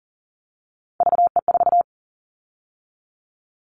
Morse code, heard as a keyed tone, about 42 wpm, 700 Hz. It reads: VE4
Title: Callsign Prefixes for Top 50 DXCC - 40wpm